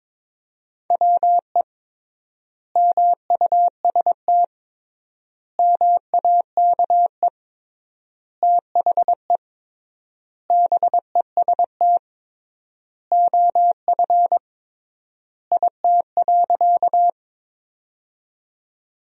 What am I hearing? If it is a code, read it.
Morse code, heard as a keyed tone, about 22 wpm, 700 Hz. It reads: WE MUST MAKE THE BEST OF IT.
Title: 3x Word-Spacing - Sentences from Top 400 Words - 22wpm